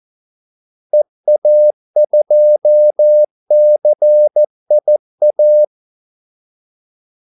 Morse code, heard as a keyed tone, about 14 wpm, 600 Hz. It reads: EA2CIA
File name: Callsigns-14wpm